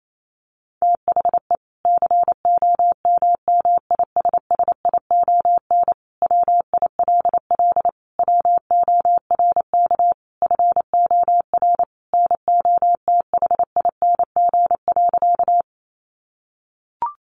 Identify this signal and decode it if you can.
Morse code, heard as a keyed tone, about 28 wpm, 700 Hz. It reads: THE COMMISSION WILL WORK FOR NOTHING.
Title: Sentences from Top 1400 Words 28wpm